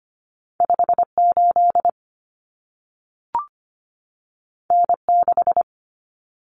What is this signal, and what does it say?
Morse code, heard as a keyed tone, about 25 wpm, 700 Hz. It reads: 58 N6
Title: Common Cut Numbers - 2-Digit 25wpm